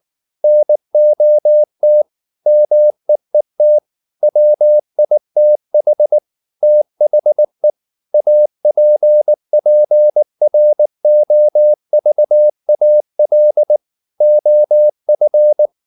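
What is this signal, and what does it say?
Morse code, heard as a keyed tone, about 19 wpm, 600 Hz. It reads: NOT MEET WITH THE APPROVAL OF
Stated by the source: Morse code at 19 wpm from CW med Gnister, reading Ken Follett's novel Never in English.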